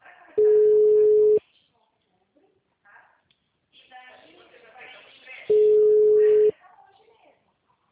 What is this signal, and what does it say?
Ringback tone (Europe, 425 Hz), with speech.
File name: out- -2017- - - .6097